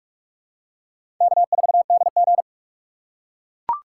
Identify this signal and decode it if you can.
Morse code, heard as a keyed tone, about 42 wpm, 700 Hz. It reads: K4DG
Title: USA Call Signs - 3x Character Spacing - 45wpm